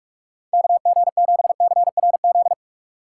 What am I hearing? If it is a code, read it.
Morse code, heard as a keyed tone, about 46 wpm, 700 Hz. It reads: KG7CRZ
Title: USA Call Signs - 6 Characters - 2x Character Spacing - 45wpm